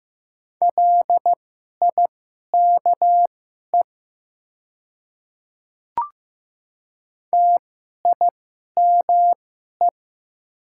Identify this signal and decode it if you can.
Morse code, heard as a keyed tone, about 15 wpm, spaced as for 11 wpm, 700 Hz. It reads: LIKE TIME
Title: 2x Character Spacing - Top 100 Words - 15wpm